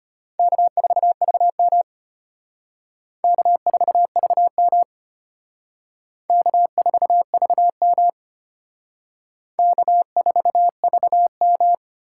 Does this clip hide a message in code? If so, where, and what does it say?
Yes: Morse code from 0.4 to 11.7 s, heard as a keyed tone, about 31 wpm, 700 Hz. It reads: K4VM K4VM K4VM K4VM